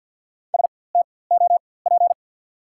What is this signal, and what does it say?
Morse code, heard as a keyed tone, about 54 wpm, spaced as for 24 wpm, 690 Hz. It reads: STOP
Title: 4x Character Spacing - Top 300 Words - 50wpm